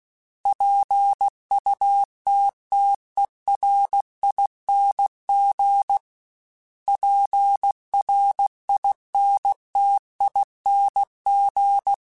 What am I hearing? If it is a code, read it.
Morse code, heard as a keyed tone, about 16 wpm, 800 Hz. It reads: PUTTERING PRINTING
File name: Special INGs / Special INGs-mp3s / Sngl ing @15